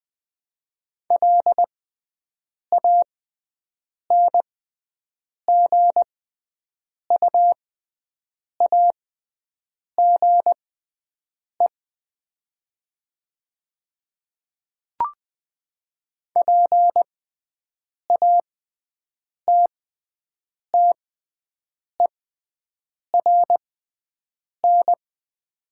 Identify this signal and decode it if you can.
Morse code, heard as a keyed tone, about 20 wpm, spaced as for 7 wpm, 700 Hz. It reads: LANGUAGE PATTERN
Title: 6x Character Spacing - Top 500 Words - 20wpm